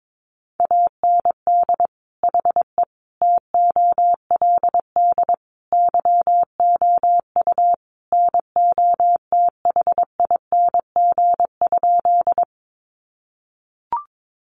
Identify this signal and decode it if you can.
Morse code, heard as a keyed tone, about 22 wpm, 700 Hz. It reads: AND HE TOLD YOU NOTHING?
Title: Sentences from Top 500 Words 22wpm